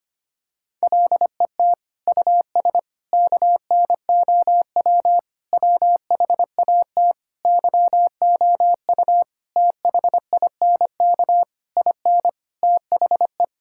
Morse code, heard as a keyed tone, about 25 wpm, 700 Hz. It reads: LET US KNOW WHAT YOU THINK IN THE
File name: practice / 25wpm / Sentences with 9-Word Phrases - 25wpm